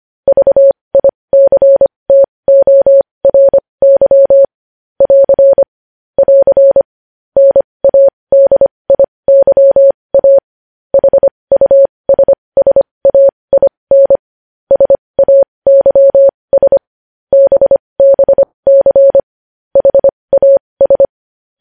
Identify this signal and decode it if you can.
Morse code, heard as a keyed tone, about 25 wpm, spaced as for 20 wpm, 550 Hz. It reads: VICTORY <AR> <AR> NADIYA HUSSAIN SAYS BBC HAS
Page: news-headlines-in-morse-code-at-20-wpm